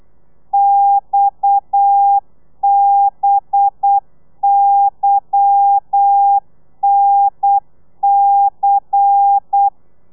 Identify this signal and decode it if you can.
Morse code, heard as a keyed tone, about 8 wpm, 800 Hz. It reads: XBYNC
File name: telegrafia / i2tfj / Lez10 35